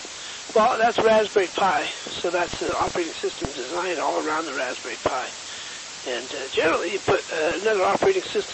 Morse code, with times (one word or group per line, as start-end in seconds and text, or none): none